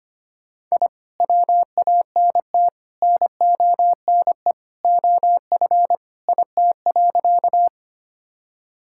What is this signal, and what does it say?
Morse code, heard as a keyed tone, about 25 wpm, 690 Hz. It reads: I WANT NONE OF IT.
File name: practice / 25wpm / Sentences with 2-Word Phrases - 25wpm